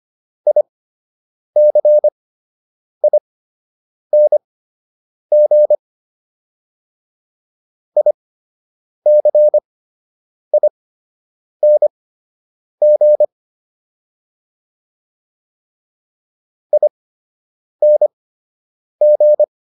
Morse code, heard as a keyed tone, about 25 wpm, spaced as for 8 wpm, 600 Hz. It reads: ICING ICING ING